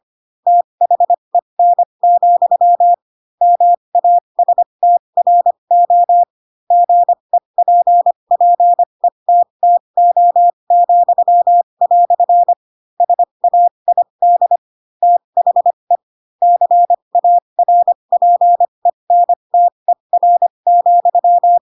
Morse code, heard as a keyed tone, about 25 wpm, spaced as for 22 wpm, 700 Hz. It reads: THEN, MASTRO GEPPETTO," SAID THE CARPENTER,
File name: Pinocchio / Pinocchio ch2 18 25 ik1hge